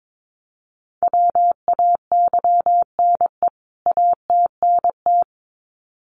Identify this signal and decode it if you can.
Morse code, heard as a keyed tone, about 22 wpm, 700 Hz. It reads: WAYNE ATNT
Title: CWT Exchange 22wpm